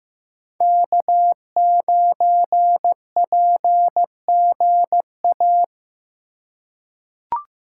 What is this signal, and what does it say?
Morse code, heard as a keyed tone, about 15 wpm, 700 Hz. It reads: K9PGA
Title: USA Call Signs - 5 Characters - 4x Character Spacing - 15wpm